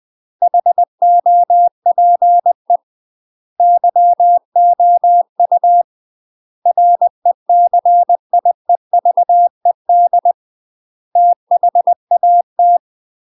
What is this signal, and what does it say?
Morse code, heard as a keyed tone, about 20 wpm, 700 Hz. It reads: HOPE YOU RECIEVED THAT
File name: UK MASTER QSOs TRAINING / 20WPM / UK MASTER QSOs 0034